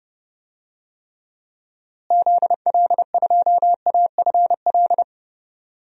Morse code, heard as a keyed tone, about 31 wpm, 700 Hz. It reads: ZL2AFL
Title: DXCC-100 Call Signs - 6 Characters - Repeated 2x - 30wpm